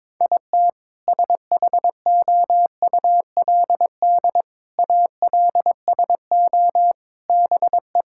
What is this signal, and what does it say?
Morse code, heard as a keyed tone, about 22 wpm, 700 Hz. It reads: IT SHOULD ALSO BE
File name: practice / 22wpm / Sets of 7 Words - 22wpm